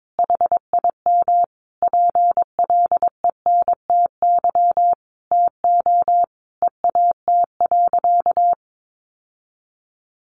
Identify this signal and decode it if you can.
Morse code, heard as a keyed tone, about 22 wpm, 700 Hz. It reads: HIM PLENTY TO EAT.
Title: Sentences from Top 1300 Words 22wpm